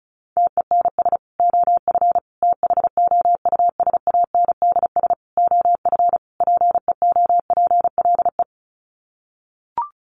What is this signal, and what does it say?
Morse code, heard as a keyed tone, about 34 wpm, 700 Hz. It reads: TENS OF THOUSANDS OF PEOPLE
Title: Sets of 5 Words Encore2 35wpm